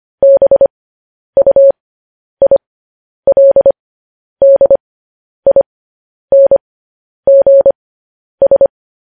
Morse code, heard as a keyed tone, about 25 wpm, spaced as for 10 wpm, 550 Hz. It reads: BUILDINGS
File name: podcast / News headlines-1x-25x10